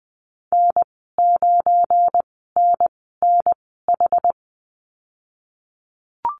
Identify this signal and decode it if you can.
Morse code, heard as a keyed tone, about 20 wpm, spaced as for 15 wpm, 700 Hz. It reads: N9NNH
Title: 186 Lesson 30g - Letter b - Call Signs 20wpm